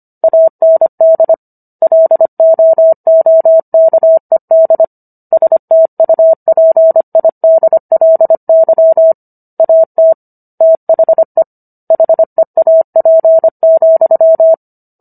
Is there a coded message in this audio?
Yes: Morse code from 0.2 to 14.5 s, heard as a keyed tone, about 25 wpm, 650 Hz. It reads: AND LOOKED STUPIDLY AT THE HEAP,